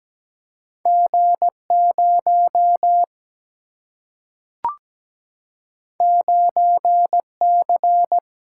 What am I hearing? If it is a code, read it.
Morse code, heard as a keyed tone, about 17 wpm, 700 Hz. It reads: G0 9C
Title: Single-Letter Single-Number 17wpm